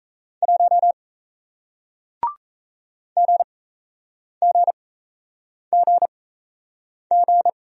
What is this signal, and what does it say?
Morse code, heard as a keyed tone, about 38 wpm, spaced as for 8 wpm, 700 Hz. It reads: 1 GGGG